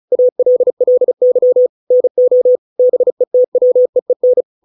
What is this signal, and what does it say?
Morse code, heard as a keyed tone, about 34 wpm, 500 Hz. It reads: ALLY NO BETWEEN